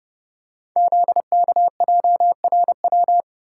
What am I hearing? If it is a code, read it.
Morse code, heard as a keyed tone, about 30 wpm, 700 Hz. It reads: ZKJRW
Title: Code Groups - 5 Letter:Numbers - 30wpm